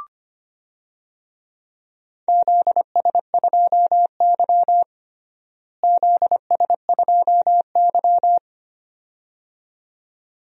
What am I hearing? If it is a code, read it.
Morse code, heard as a keyed tone, about 25 wpm, 700 Hz. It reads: ZS2Y ZS2Y
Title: DXCC-100 Call Signs - 4 Characters - Repeated 2x - 25wpm